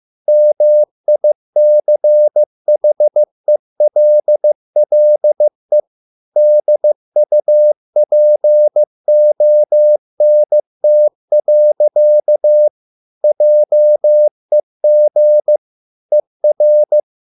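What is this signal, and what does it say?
Morse code, heard as a keyed tone, about 15 wpm, 600 Hz. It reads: MICHELLE DUPONT. JEG ER
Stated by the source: Morse code at a slow 15 wpm from CW med Gnister.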